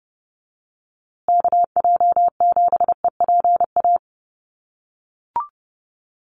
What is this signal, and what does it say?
Morse code, heard as a keyed tone, about 30 wpm, 700 Hz. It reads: KJ7EPA
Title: USA Call Signs Repeated 2x 30wpm